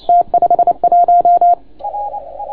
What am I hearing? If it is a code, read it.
Morse code, heard as a keyed tone, about 28 wpm, 690 Hz. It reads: T51